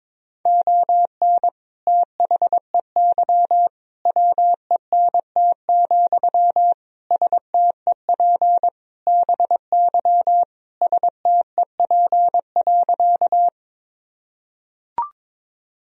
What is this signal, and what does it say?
Morse code, heard as a keyed tone, about 22 wpm, 700 Hz. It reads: ON THEY WENT, STEP BY STEP.
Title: Sentences from Top 1000 Words - Review Part 2 - 22wpm